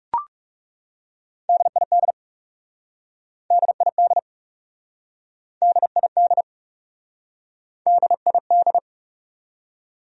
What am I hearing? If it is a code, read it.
Morse code, heard as a keyed tone, about 40 wpm, 700 Hz. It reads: DID DID DID DID